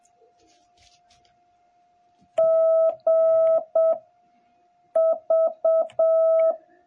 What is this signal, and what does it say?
Morse code, heard as a keyed tone, about 7 wpm, 660 Hz. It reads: G V